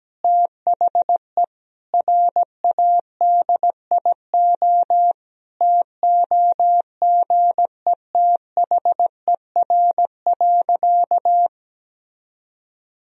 Morse code, heard as a keyed tone, about 17 wpm, 700 Hz. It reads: THE RADIO TOGETHER.
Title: Sentences from Top 1000 Words - Review Part 3 - 17wpm